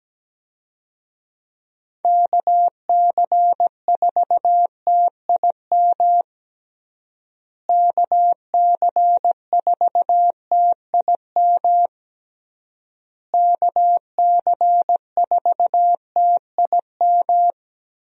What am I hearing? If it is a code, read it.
Morse code, heard as a keyed tone, about 17 wpm, 700 Hz. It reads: KC4TIM KC4TIM KC4TIM
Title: USA Call Signs - 6 Characters - Repeated 3x - 17wpm